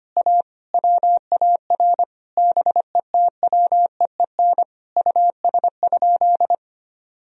Morse code, heard as a keyed tone, about 25 wpm, 700 Hz. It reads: A WAR BETWEEN US?
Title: Sentences from Top 1000 Words - Review Part 2 - 25wpm